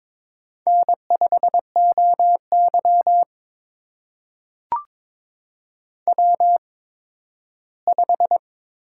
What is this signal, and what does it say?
Morse code, heard as a keyed tone, about 22 wpm, spaced as for 13 wpm, 700 Hz. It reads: N5OY W5
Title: USA Call Signs - 4 Characters - 8x Character Spacing - 22wpm